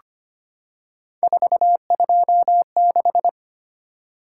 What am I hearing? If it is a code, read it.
Morse code, heard as a keyed tone, about 25 wpm, 700 Hz. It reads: 426
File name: Three Letter-Number Combinations - 25wpm